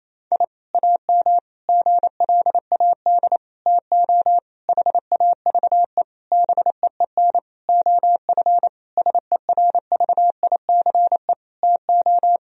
Morse code, heard as a keyed tone, about 28 wpm, 700 Hz. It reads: I AM GLAD TO HAVE BEEN OF SERVICE TO
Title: Sentences from Top 1000 Words - Review Part 4 - 28wpm